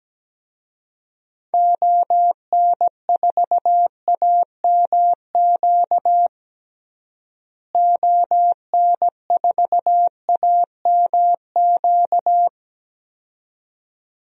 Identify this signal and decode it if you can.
Morse code, heard as a keyed tone, about 17 wpm, 700 Hz. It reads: ON4AMQ ON4AMQ
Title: DXCC-100 Call Signs - 6 Characters - Repeated 2x - 17wpm